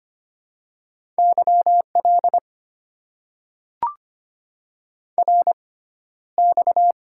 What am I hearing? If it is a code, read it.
Morse code, heard as a keyed tone, about 25 wpm, spaced as for 13 wpm, 700 Hz. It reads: YL RX
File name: Top 90 QSO Elements - 6x Character Spacing - 25wpm